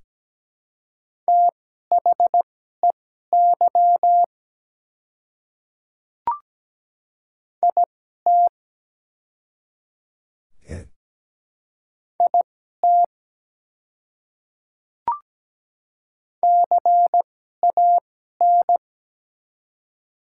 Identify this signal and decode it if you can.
Morse code, heard as a keyed tone, about 17 wpm, spaced as for 12 wpm, 700 Hz. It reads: THEY IT IT CAN
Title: 2x Character Spacing - Top 100 Words - 17wpm